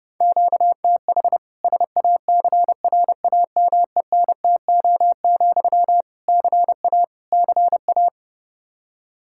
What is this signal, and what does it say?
Morse code, heard as a keyed tone, about 30 wpm, 700 Hz. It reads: QTH SACRAMENTO, CA CA
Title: K1USN SST QSO Practice 30wpm